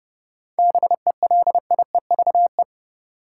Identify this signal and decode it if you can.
Morse code, heard as a keyed tone, about 31 wpm, 700 Hz. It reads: BELIEVE
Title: Sets of 4 Words - New Word Drill - 30wpm